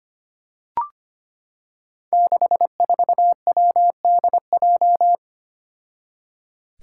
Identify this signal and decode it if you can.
Morse code, heard as a keyed tone, about 25 wpm, 700 Hz. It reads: 64WDJ
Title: Five Challenging Letter-Number Combinations 25wpm